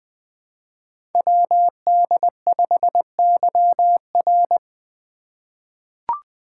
Morse code, heard as a keyed tone, about 20 wpm, 700 Hz. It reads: WD5YR
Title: Code Groups - 5 Letter:Numbers - 20wpm